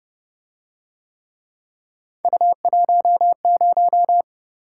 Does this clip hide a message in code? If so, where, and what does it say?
Yes: Morse code from 2.2 to 4.2 s, heard as a keyed tone, about 29 wpm, 700 Hz. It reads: U10